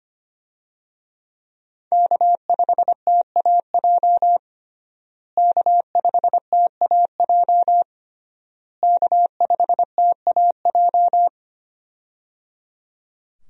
Morse code, heard as a keyed tone, about 25 wpm, 700 Hz. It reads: K5TAJ K5TAJ K5TAJ
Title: USA Call Signs - 5 Characters - Repeated 3x - 25wpm